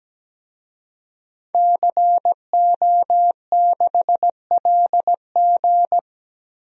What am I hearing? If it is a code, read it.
Morse code, heard as a keyed tone, about 17 wpm, 700 Hz. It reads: CO6LG
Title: DXCC-100 Call Signs - 5 Characters - Repeated 3x - 17wpm